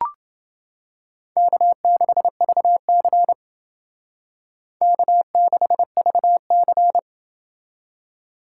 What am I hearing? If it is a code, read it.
Morse code, heard as a keyed tone, about 29 wpm, 700 Hz. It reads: K6VC K6VC